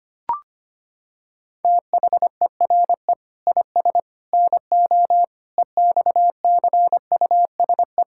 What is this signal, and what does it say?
Morse code, heard as a keyed tone, about 25 wpm, 700 Hz. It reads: THERE IS NO EXCUSE
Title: Sentences from Top 1000 Words - Review Part 6 - 25wpm